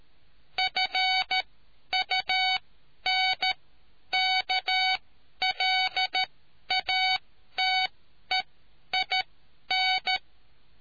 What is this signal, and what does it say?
Morse code, heard as a keyed tone, about 13 wpm, spaced as for 10 wpm, 750 Hz. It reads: FUNKLATEIN